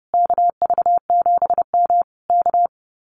Morse code, heard as a keyed tone, about 30 wpm, 700 Hz. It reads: KV7M K
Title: Common QSO Phrases Encore 30wpm